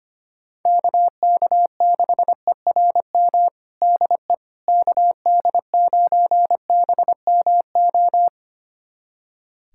Morse code, heard as a keyed tone, about 25 wpm, 700 Hz. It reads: KK6ERM DE KD9BMO
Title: USA Callsign DE Callsign - 6 Characters - 25wpm